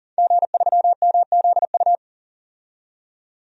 Morse code, heard as a keyed tone, about 40 wpm, 700 Hz. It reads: G3MZU
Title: DXCC-100 Call Signs - 5 Characters - 40wpm